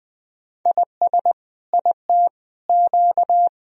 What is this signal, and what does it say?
Morse code, heard as a keyed tone, about 21 wpm, 700 Hz. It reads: IS IT Q
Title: Sentences from Top 1000 Words - Review Part 3 - 20wpm